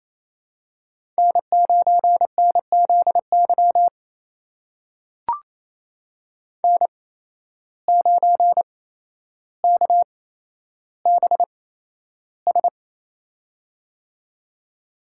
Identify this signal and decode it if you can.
Morse code, heard as a keyed tone, about 28 wpm, spaced as for 12 wpm, 700 Hz. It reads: N9NZY N9KBS
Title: USA Call Signs - 5 Characters - 8x Character Spacing - 28wpm